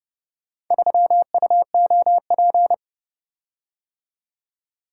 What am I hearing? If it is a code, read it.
Morse code, heard as a keyed tone, about 29 wpm, 700 Hz. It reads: 3UOP